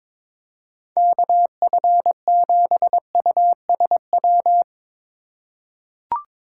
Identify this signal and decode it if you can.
Morse code, heard as a keyed tone, about 21 wpm, 700 Hz. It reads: KF7USW
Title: USA Call Signs - 6 Characters - 4x Character Spacing - 22wpm